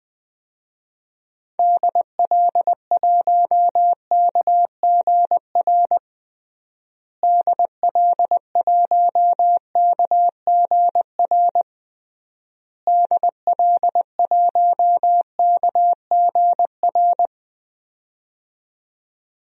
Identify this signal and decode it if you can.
Morse code, heard as a keyed tone, about 20 wpm, 700 Hz. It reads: DL1KGR DL1KGR DL1KGR
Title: DXCC-100 Call Signs - 6 Characters - Repeated 3x - 20wpm